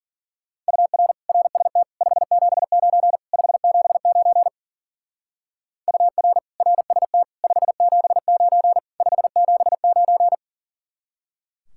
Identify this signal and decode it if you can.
Morse code, heard as a keyed tone, about 48 wpm, 700 Hz. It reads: UR RST 579 579 UR RST 579 579
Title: Speed-Racing Common QSO Phrases - 40wpm